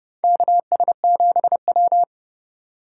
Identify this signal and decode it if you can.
Morse code, heard as a keyed tone, about 32 wpm, 700 Hz. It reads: KS7W
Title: USA Call Signs - 4 Characters - 2x Character Spacing - 30wpm